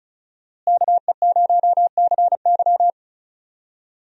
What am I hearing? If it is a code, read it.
Morse code, heard as a keyed tone, about 35 wpm, 700 Hz. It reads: KE0CY
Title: USA Call Signs - 5 Characters - Repeated 2x - 35wpm